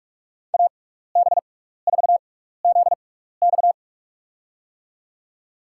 Morse code, heard as a keyed tone, about 46 wpm, spaced as for 16 wpm, 700 Hz. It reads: AB4ZX
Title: USA Call Signs - 5 Characters - 6x Character Spacing - 45wpm